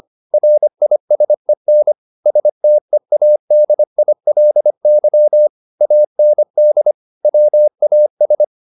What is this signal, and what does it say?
Morse code, heard as a keyed tone, about 25 wpm, 600 Hz. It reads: RISEN STEADILY AND WAS